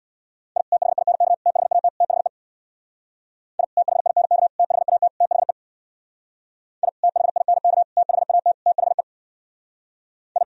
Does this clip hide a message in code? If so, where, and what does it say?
Yes: Morse code from 0.6 to 10.4 s, heard as a keyed tone, about 64 wpm, 700 Hz. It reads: I THINK THAT THE I THINK THAT THE I THINK THAT THE I